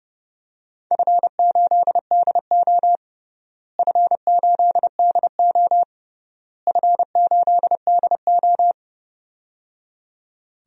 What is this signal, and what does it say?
Morse code, heard as a keyed tone, about 30 wpm, 700 Hz. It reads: F8DO F8DO F8DO
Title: DXCC-100 Call Signs - 4 Characters - Repeated 3x - 30wpm